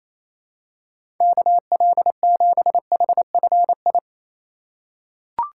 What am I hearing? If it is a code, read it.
Morse code, heard as a keyed tone, about 27 wpm, 700 Hz. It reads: KL7HFI
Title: K1USN SST QSO Practice - 28wpm